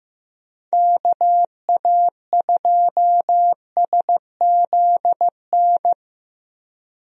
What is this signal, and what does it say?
Morse code, heard as a keyed tone, about 15 wpm, 700 Hz. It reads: KA2SZN
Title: USA Call Signs - 6 Characters - Repeated 3x - 15wpm